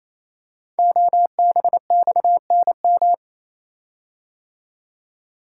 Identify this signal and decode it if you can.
Morse code, heard as a keyed tone, about 27 wpm, 690 Hz. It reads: OBXNM
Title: Code Groups - 5 Letters - 28wpm